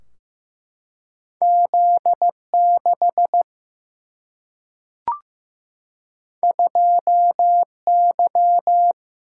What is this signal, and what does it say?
Morse code, heard as a keyed tone, about 15 wpm, 700 Hz. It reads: Z 6 2 Y
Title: Single-Letter Single-Number 15wpm